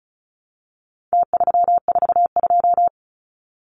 Morse code, heard as a keyed tone, about 34 wpm, 700 Hz. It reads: T342
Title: Numbers - Contest Style - 4-Digit 35wpm